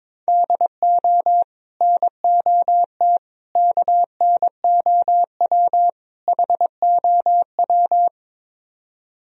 Morse code, heard as a keyed tone, about 22 wpm, 700 Hz. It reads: DO NOT KNOW HOW
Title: Speed-Racing - Sets of 4 Words 22wpm